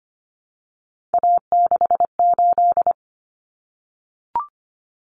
Morse code, heard as a keyed tone, about 25 wpm, 700 Hz. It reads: A68
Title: Common Cut Numbers - 3-Digit 25wpm